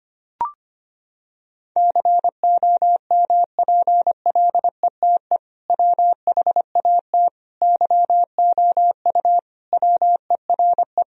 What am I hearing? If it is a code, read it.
Morse code, heard as a keyed tone, about 25 wpm, 700 Hz. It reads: COMPLETE WHAT YOU WERE
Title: Sentences from Top 1000 Words - Review Part 2 - 25wpm